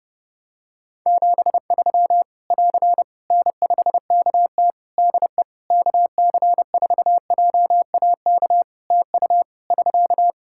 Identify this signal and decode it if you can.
Morse code, heard as a keyed tone, about 30 wpm, 700 Hz. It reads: 73 <AR> N5KT DE KC4JAK TU <SK>
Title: K1USN SST QSO Practice 30wpm